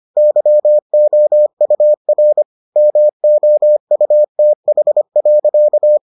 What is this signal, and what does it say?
Morse code, heard as a keyed tone, about 25 wpm, 600 Hz. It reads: YOUR MOUTH.